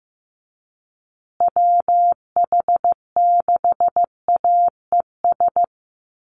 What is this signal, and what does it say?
Morse code, heard as a keyed tone, about 15 wpm, 700 Hz. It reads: WH6AES
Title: USA Call Signs Repeated 3x 15wpm